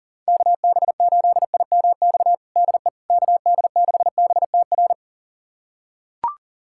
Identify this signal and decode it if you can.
Morse code, heard as a keyed tone, about 40 wpm, 700 Hz. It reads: KB8IMX DE KD6BTR
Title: USA Callsign DE Callsign 40wpm